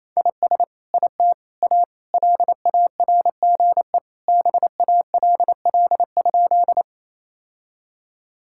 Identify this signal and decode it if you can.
Morse code, heard as a keyed tone, about 28 wpm, 700 Hz. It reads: IS IT A LARGE BALL?
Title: Sentences from Top 1000 Words - Review Part 6 - 28wpm